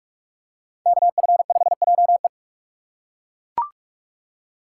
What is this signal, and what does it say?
Morse code, heard as a keyed tone, about 42 wpm, 700 Hz. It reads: KF5JE
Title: USA Call Signs - 6x Character Spacing - 45wpm